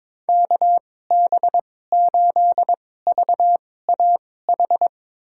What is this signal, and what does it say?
Morse code, heard as a keyed tone, about 22 wpm, spaced as for 16 wpm, 700 Hz. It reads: KB8VAH
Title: USA Call Signs - 2x Character Spacing - 22wpm